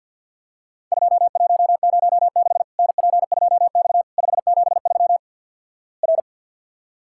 Morse code, heard as a keyed tone, about 49 wpm, 700 Hz. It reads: 2106 NP2X 573 R
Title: Worked All Europe DX Contest - QTC Exchange with Repeats - 50wpm